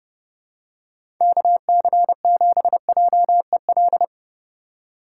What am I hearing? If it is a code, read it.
Morse code, heard as a keyed tone, about 30 wpm, 700 Hz. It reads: KC7JEL
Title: USA Call Signs - 4x Character Spacing - 30wpm